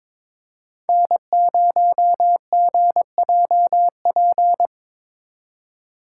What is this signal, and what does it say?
Morse code, heard as a keyed tone, about 22 wpm, 700 Hz. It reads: N0GJP
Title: USA Call Signs - 5 Characters - 4x Character Spacing - 22wpm